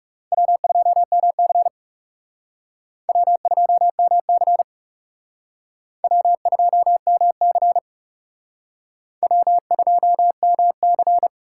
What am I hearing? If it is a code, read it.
Morse code, heard as a keyed tone, about 37 wpm, 700 Hz. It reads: W2MC W2MC W2MC W2MC
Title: Speed Racing - USA 1x2 Call Signs 30wpm